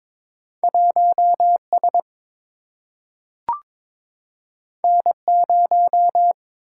Morse code, heard as a keyed tone, about 22 wpm, 700 Hz. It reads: 1 S N 0
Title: Single-Letter Single-Number 22wpm